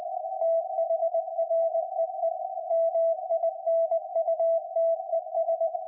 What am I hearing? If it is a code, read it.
Morse code, heard as a keyed tone, about 20 wpm, 660 Hz, with radio noise. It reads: THREE MINUTES
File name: IC-7100